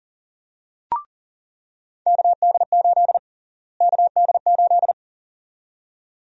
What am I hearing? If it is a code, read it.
Morse code, heard as a keyed tone, about 40 wpm, 700 Hz. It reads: KD8 KD8
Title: USA Call Sign Prefixes - Repeated 2x - 40wpm